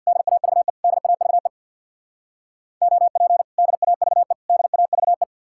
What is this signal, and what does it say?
Morse code, heard as a keyed tone, about 61 wpm, 700 Hz. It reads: DAVE DAVE OP DAVE DAVE
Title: Speed-Racing Common QSO Phrases - 50wpm